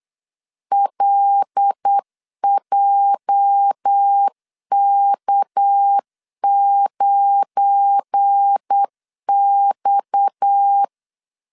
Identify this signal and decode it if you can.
Morse code, heard as a keyed tone, about 8 wpm, 800 Hz. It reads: LJK9X
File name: LEZIONI / 8 YJ45 / 8 YJ 8wpm 3313